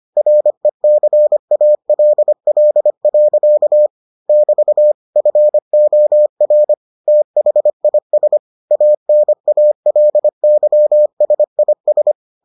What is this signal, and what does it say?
Morse code, heard as a keyed tone, about 25 wpm, 600 Hz. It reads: RECALL. <BT> FOR THIS ANALYSIS